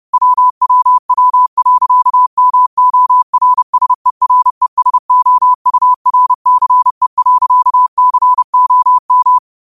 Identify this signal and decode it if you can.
Morse code, heard as a keyed tone, about 30 wpm, 1000 Hz. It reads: WWW.MORSERESOURCE.COM
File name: qotdfiles / 30wpm